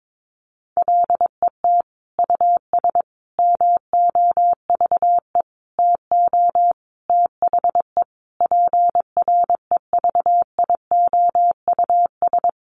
Morse code, heard as a keyed tone, about 22 wpm, 700 Hz. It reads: LET US MOVE TO THE PREVIOUS
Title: Sentences from Top 1500 Words 22wpm